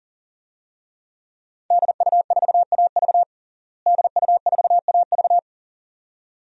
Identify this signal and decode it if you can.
Morse code, heard as a keyed tone, about 40 wpm, 700 Hz. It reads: DU4AV DU4AV
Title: DXCC-100 Call Signs - 5 Characters - Repeated 2x - 40wpm